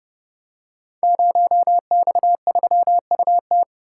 Morse code, heard as a keyed tone, about 31 wpm, 700 Hz. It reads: 0X3UT
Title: Code Groups - 5 Letter:Numbers - 30wpm